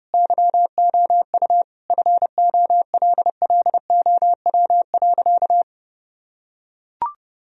Morse code, heard as a keyed tone, about 30 wpm, 700 Hz. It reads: YOU FOLLOW.
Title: Sentences from Top 1000 Words - Review Part 3 - 30wpm